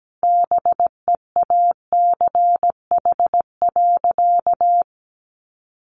Morse code, heard as a keyed tone, about 17 wpm, 700 Hz. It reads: BEACH.
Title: Sentences from Top 1500 Words 17wpm